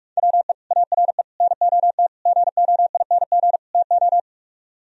Morse code, heard as a keyed tone, about 50 wpm, 700 Hz. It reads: WE ARE NOT GOING TO